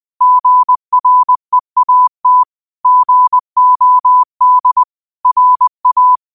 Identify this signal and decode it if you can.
Morse code, heard as a keyed tone, about 20 wpm, 1000 Hz. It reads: GREAT GOD RA